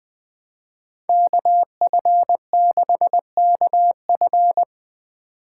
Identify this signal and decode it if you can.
Morse code, heard as a keyed tone, about 20 wpm, 700 Hz. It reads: KF6KF
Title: Speed Racing - USA 2x2 Call Signs 17wpm